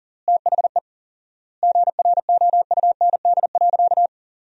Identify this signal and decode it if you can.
Morse code, heard as a keyed tone, about 40 wpm, 700 Hz. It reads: THE GROUND.
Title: 4x Word-Spacing - Sentences from Top 500 Words - 40wpm